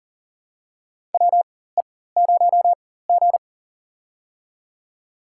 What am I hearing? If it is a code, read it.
Morse code, heard as a keyed tone, about 37 wpm, spaced as for 18 wpm, 700 Hz. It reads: WE0G
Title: USA Call Signs - 4 Characters - 4x Character Spacing - 40wpm